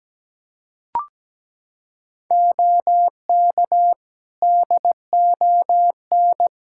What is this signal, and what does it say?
Morse code, heard as a keyed tone, about 17 wpm, 700 Hz. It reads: OK DON
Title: Practice QSO - Exchange 3 - More Info - 17wpm